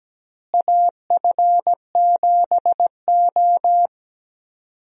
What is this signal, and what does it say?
Morse code, heard as a keyed tone, about 17 wpm, 700 Hz. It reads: AF7O
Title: USA Call Signs - 4 Characters - Repeated 2x - 17wpm